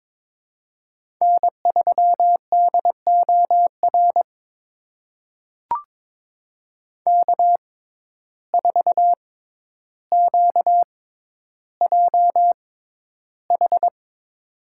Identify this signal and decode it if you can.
Morse code, heard as a keyed tone, about 22 wpm, spaced as for 11 wpm, 700 Hz. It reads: N3DOR K4QJH
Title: USA Call Signs - 5 Characters - 6x Character Spacing - 22wpm